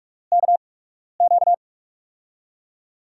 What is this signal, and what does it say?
Morse code, heard as a keyed tone, about 44 wpm, spaced as for 12 wpm, 700 Hz. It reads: KQ
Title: USA Call Signs - 8x Character Spacing - 45wpm